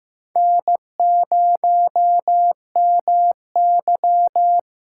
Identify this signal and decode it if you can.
Morse code, heard as a keyed tone, about 15 wpm, 700 Hz. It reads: N0MY
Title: USA Call Signs - 4 Characters - 6x Character Spacing - 15wpm